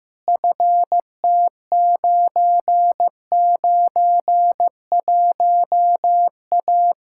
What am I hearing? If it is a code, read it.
Morse code, heard as a keyed tone, about 15 wpm, 700 Hz. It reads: FT991A
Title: Practice QSO - Exchange 2 - Info - 15wpm